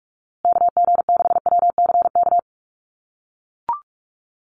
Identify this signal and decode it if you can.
Morse code, heard as a keyed tone, about 44 wpm, 690 Hz. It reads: KG6WCK
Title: USA Call Signs Repeated 3x 45wpm